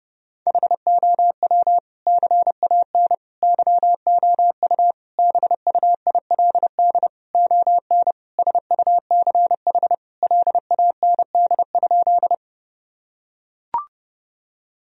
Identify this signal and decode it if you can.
Morse code, heard as a keyed tone, about 30 wpm, 700 Hz. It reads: HOW CAN YOU BUILD ON SUCH LAND?
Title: Sentences from Top 1000 Words - Review Part 1 - 30wpm